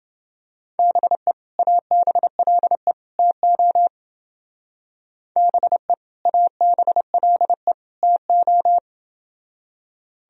Morse code, heard as a keyed tone, about 30 wpm, 700 Hz. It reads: BE ABLE TO BE ABLE TO